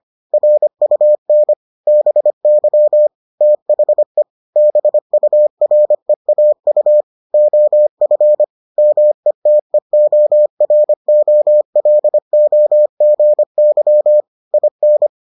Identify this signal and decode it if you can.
Morse code, heard as a keyed tone, about 25 wpm, 600 Hz. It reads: RUN BY THE BUREAU OF METEOROLOGY IN